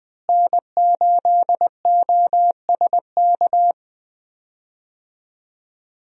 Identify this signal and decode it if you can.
Morse code, heard as a keyed tone, about 20 wpm, 700 Hz. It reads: N8OSK
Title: USA Call Signs - 5 Characters - Repeated 2x - 20wpm